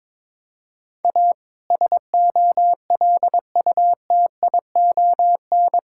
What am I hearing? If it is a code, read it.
Morse code, heard as a keyed tone, about 22 wpm, 700 Hz. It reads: A SOLUTION
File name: practice / 22wpm / Sets of 7 Words - 22wpm